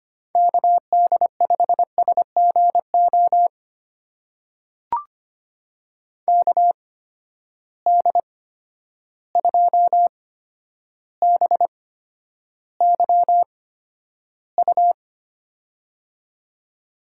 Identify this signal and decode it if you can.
Morse code, heard as a keyed tone, about 25 wpm, spaced as for 11 wpm, 700 Hz. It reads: KD5SGO KD2BYU
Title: USA Call Signs - 6 Characters - 8x Character Spacing - 25wpm